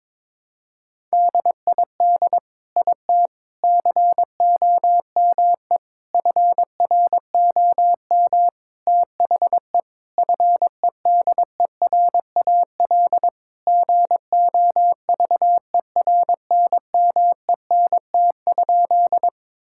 Morse code, heard as a keyed tone, about 22 wpm, 700 Hz. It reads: DID IT COME FROM THE FEDERAL GOVERNMENT?
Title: Sentences from Top 1000 Words - Review Part 5 - 22wpm